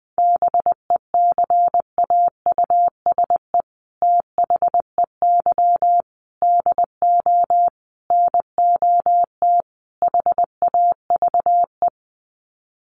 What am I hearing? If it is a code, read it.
Morse code, heard as a keyed tone, about 19 wpm, 700 Hz. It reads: BECAUSE THEY DO NOT HAVE
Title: Sets of 5 Words Encore2 20wpm